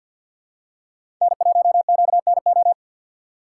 Morse code, heard as a keyed tone, about 50 wpm, 690 Hz. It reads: N1QNO
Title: USA Call Signs - 4x Character Spacing - 50wpm